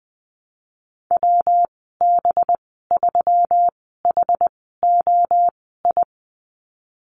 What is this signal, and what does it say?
Morse code, heard as a keyed tone, about 20 wpm, spaced as for 15 wpm, 700 Hz. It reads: WB3HOI
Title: 186 Lesson 30g - Letter b - Call Signs 20wpm